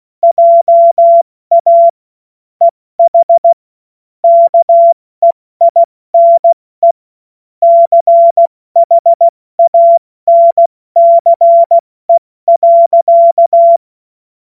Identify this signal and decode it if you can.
Morse code, heard as a keyed tone, about 16 wpm, 680 Hz. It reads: JA EH KEINE CHANCE.
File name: Morsekurs / Witze / Witz 02. Februar 2021, 70 F80